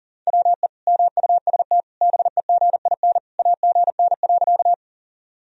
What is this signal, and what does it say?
Morse code, heard as a keyed tone, about 40 wpm, 700 Hz. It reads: WE MUST BEGIN AGN.
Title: Sentences from Top 300 Words and Abbreviations 40wpm